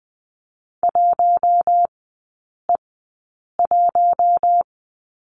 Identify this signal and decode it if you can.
Morse code, heard as a keyed tone, about 20 wpm, spaced as for 8 wpm, 700 Hz. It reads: 1E1
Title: 041 Lesson 8e - Number 1 - Three Characters 20wpm